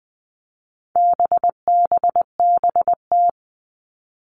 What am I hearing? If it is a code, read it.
Morse code, heard as a keyed tone, about 20 wpm, 700 Hz. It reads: BBBT
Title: All Cut Numbers - 4-Digit 20wpm